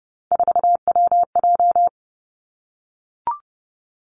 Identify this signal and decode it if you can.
Morse code, heard as a keyed tone, about 30 wpm, 700 Hz. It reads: 4WJ
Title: Three Challenging Letter-Number Combinations 30wpm